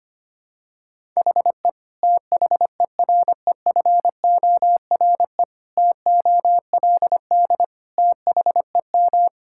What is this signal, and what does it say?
Morse code, heard as a keyed tone, about 25 wpm, 700 Hz. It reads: HE THEREFORE TOLD THEM
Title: Sentences from Top 1000 Words - Review Part 5 - 25wpm